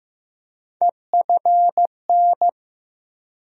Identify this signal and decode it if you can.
Morse code, heard as a keyed tone, about 15 wpm, 700 Hz. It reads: EFN
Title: USA Call Sign Postfixes - 3 Letters 15wpm